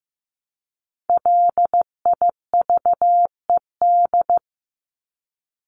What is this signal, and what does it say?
Morse code, heard as a keyed tone, about 15 wpm, 700 Hz. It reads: LIVED
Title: ED Suffix - Top 100 Words with Base Words 15wpm